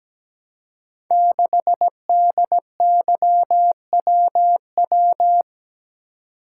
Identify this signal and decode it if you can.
Morse code, heard as a keyed tone, about 17 wpm, 700 Hz. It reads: 6DYWW
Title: Code Groups - 5 Letter:Numbers - 17wpm